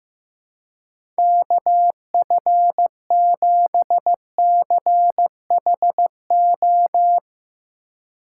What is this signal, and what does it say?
Morse code, heard as a keyed tone, about 15 wpm, 700 Hz. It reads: KF7CHO
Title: USA Call Signs - 6 Characters - 3x Character Spacing - 15wpm